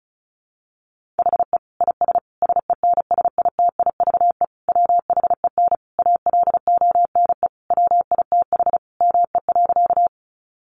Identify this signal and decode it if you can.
Morse code, heard as a keyed tone, about 34 wpm, 700 Hz. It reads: HE IS SENSITIVE WHEN ALONE WITH ME.
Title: Sentences from Top 1400 Words 35wpm